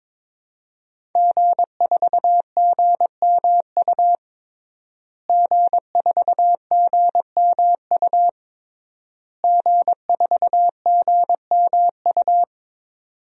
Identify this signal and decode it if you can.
Morse code, heard as a keyed tone, about 22 wpm, 700 Hz. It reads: G4GMU G4GMU G4GMU
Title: DXCC-100 Call Signs - 5 Characters - Repeated 3x - 22wpm